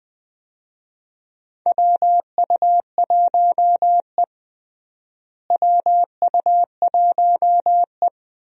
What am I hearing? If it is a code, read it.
Morse code, heard as a keyed tone, about 20 wpm, 700 Hz. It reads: WU1E WU1E
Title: USA Call Signs - 4 Characters - Repeated 3x - 20wpm